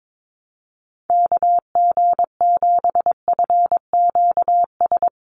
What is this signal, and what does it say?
Morse code, heard as a keyed tone, about 22 wpm, 700 Hz. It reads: KG7FQS
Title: USA Call Signs 22wpm